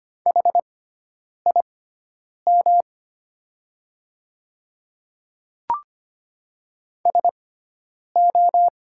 Morse code, heard as a keyed tone, about 25 wpm, spaced as for 9 wpm, 690 Hz. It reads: HIM SO
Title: 6x Character Spacing - Top 100 Words - 25wpm